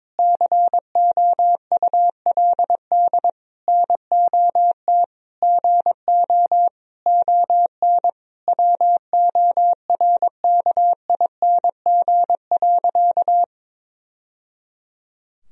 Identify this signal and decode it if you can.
Morse code, heard as a keyed tone, about 22 wpm, 700 Hz. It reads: COULD NOT GO ON WORKING.
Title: Sentences from Top 1000 Words - Review Part 4 - 22wpm